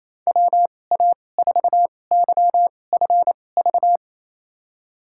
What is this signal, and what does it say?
Morse code, heard as a keyed tone, about 28 wpm, spaced as for 20 wpm, 700 Hz. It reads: WA4YFV
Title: USA Call Signs - 6 Characters - 2x Character Spacing - 28wpm